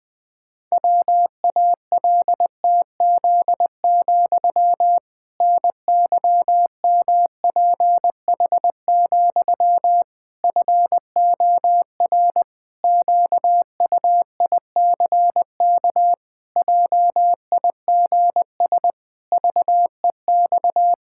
Morse code, heard as a keyed tone, about 20 wpm, 700 Hz. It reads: WALTZ, NYMPH, FOR QUICK JIGS VEX
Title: Pangram Challenge v2 - 20wpm